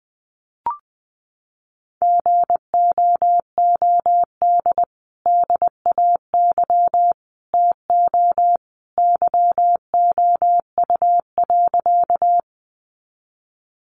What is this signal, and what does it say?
Morse code, heard as a keyed tone, about 20 wpm, 700 Hz. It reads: GOOD DAY TO YOU.
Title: Sentences from Top 200 Words 20wpm